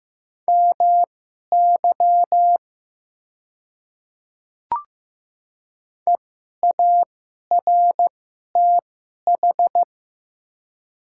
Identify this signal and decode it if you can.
Morse code, heard as a keyed tone, about 15 wpm, spaced as for 11 wpm, 700 Hz. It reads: MY EARTH
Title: 2x Character Spacing - Top 200 Words - 15wpm